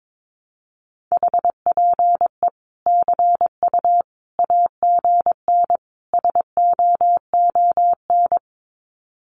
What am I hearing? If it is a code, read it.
Morse code, heard as a keyed tone, about 22 wpm, 700 Hz. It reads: HPE CU AGN SOON
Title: Common QSO Phrases 22wpm